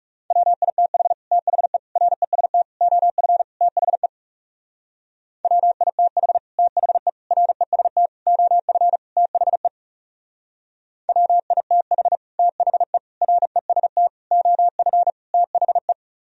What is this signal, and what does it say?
Morse code, heard as a keyed tone, about 36 wpm, 700 Hz. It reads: WITH THE REST OF THE WITH THE REST OF THE WITH THE REST OF THE
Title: Speed-Racing - Sets of 5 Words 30wpm